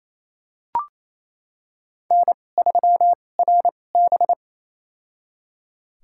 Morse code, heard as a keyed tone, about 28 wpm, spaced as for 20 wpm, 700 Hz. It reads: N3RB
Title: USA Call Signs - 2x Character Spacing - 28wpm